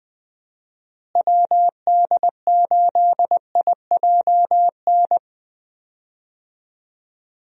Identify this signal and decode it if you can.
Morse code, heard as a keyed tone, about 20 wpm, 700 Hz. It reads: WD8IJN
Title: USA Call Signs - 6 Characters - Repeated 2x - 20wpm